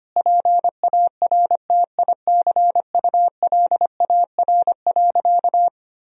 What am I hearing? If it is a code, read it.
Morse code, heard as a keyed tone, about 25 wpm, 700 Hz. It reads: PARTICULAR.
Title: Sentences from Top 1000 Words - Review Part 4 - 25wpm